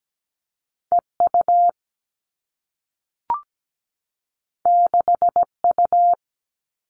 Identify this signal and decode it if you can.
Morse code, heard as a keyed tone, about 17 wpm, 700 Hz. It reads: E U 6 U
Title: Two Challenging Letter-Number Combinations 17wpm